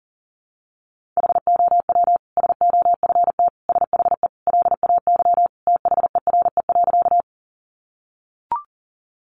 Morse code, heard as a keyed tone, about 40 wpm, 700 Hz. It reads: HOW SOFT SHE LAY THERE.
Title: Sentences from Top 1500 Words 40wpm